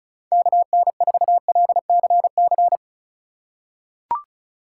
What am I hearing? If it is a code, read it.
Morse code, heard as a keyed tone, about 34 wpm, 700 Hz. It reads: KN4LCC
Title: USA Call Signs - 6x Character Spacing - 35wpm